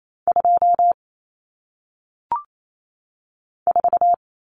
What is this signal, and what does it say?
Morse code, heard as a keyed tone, about 28 wpm, 700 Hz. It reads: 2 4
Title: Numbers - 1-Digit 28wpm